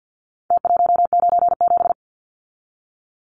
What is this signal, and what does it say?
Morse code, heard as a keyed tone, about 50 wpm, 700 Hz. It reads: T197
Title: Numbers - Contest Style - 4-Digit 50wpm